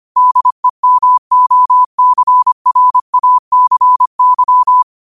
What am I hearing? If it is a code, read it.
Morse code, heard as a keyed tone, about 25 wpm, 1000 Hz. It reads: DEMOCRACY
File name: qotdfiles / 25wpm